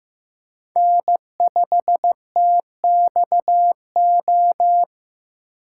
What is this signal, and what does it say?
Morse code, heard as a keyed tone, about 15 wpm, 700 Hz. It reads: N5TXO
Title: USA Call Signs - 5 Characters - 4x Character Spacing - 15wpm